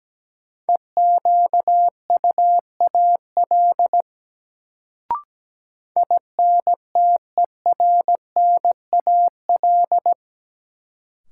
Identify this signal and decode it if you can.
Morse code, heard as a keyed tone, about 17 wpm, 700 Hz. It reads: EQUAL INTERNAL
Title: Sentences from Top 1400 Words - New Word Drill Alt - 17wpm